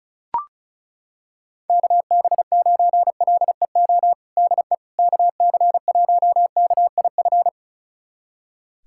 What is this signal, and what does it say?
Morse code, heard as a keyed tone, about 36 wpm, 690 Hz. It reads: KB9LEO DE KC1KIF
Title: USA Callsign DE Callsign - 6 Characters - 35wpm